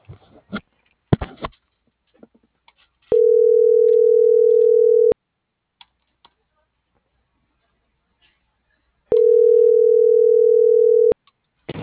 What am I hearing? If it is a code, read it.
Ringback tone (North America), 440 + 480 Hz.